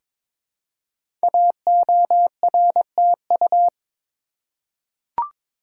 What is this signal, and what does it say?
Morse code, heard as a keyed tone, about 23 wpm, 700 Hz. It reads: AORTU
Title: Code Groups - 5 Letters - 22wpm